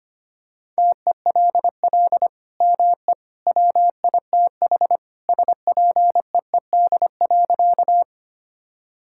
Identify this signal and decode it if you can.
Morse code, heard as a keyed tone, about 25 wpm, 700 Hz. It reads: TELL ME WITH SPEED.
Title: Sentences from Top 1000 Words - Review Part 5 - 25wpm